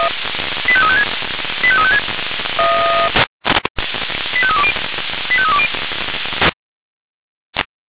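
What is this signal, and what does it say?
Five-tone selective call (ZVEI), heard as quick runs of five tones, tones of about 70 ms. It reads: 85356 85356 85320 85320
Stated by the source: Five-tone ZVEI selective call received by SDR.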